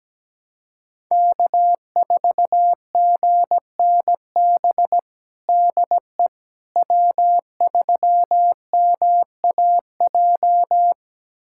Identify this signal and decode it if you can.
Morse code, heard as a keyed tone, about 17 wpm, 700 Hz. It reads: K4GNB DE W3MAJ
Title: USA Callsign DE Callsign - 5 Characters - 17wpm